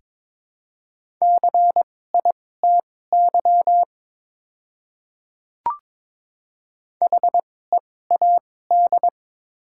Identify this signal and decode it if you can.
Morse code, heard as a keyed tone, about 22 wpm, spaced as for 16 wpm, 700 Hz. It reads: CITY HEAD
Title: 2x Character Spacing - Top 500 Words - 22wpm